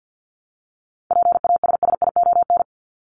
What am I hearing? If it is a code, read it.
Morse code, heard as a keyed tone, about 50 wpm, 700 Hz. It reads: PASSION
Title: Top 1001-1250 Words 50wpm